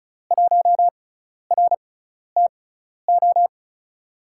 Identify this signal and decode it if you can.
Morse code, heard as a keyed tone, about 36 wpm, spaced as for 12 wpm, 700 Hz. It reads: 1RTO
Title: USA Call Signs - 6x Character Spacing - 35wpm